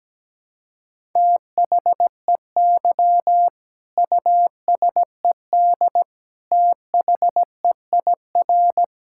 Morse code, heard as a keyed tone, about 17 wpm, 700 Hz. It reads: THEY USED THEIR
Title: Sentences from Top 1000 Words - Review Part 6 - 17wpm